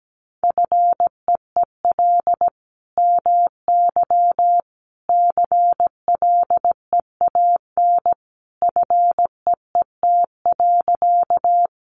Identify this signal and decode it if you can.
Morse code, heard as a keyed tone, about 17 wpm, 700 Hz. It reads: FEEL MY CLEAN FEET.
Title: Sentences from Top 1100 Words 17wpm